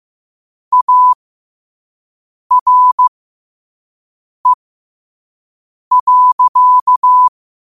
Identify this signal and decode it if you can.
Morse code, heard as a keyed tone, about 15 wpm, spaced as for 5 wpm, 1000 Hz. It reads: ARE.
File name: qotdfiles / 5wpm